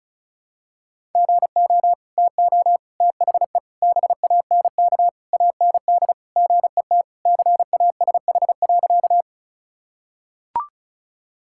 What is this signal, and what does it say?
Morse code, heard as a keyed tone, about 34 wpm, 700 Hz. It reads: GO TO THE BANK AND GET CASH.
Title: Sentences from Top 1000 Words - Review Part 5 - 35wpm